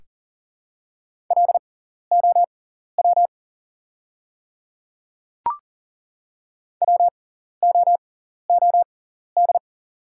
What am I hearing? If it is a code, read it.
Morse code, heard as a keyed tone, about 40 wpm, spaced as for 14 wpm, 690 Hz. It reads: LOW WOOD
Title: 6x Character Spacing - Top 300 Words - 40wpm